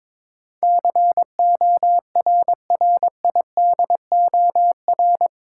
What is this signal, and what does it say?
Morse code, heard as a keyed tone, about 22 wpm, 700 Hz. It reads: CORRIDOR
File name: practice / 22wpm / Eight Letter Words - 22wpm